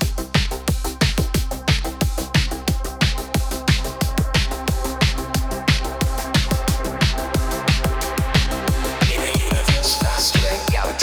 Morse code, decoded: none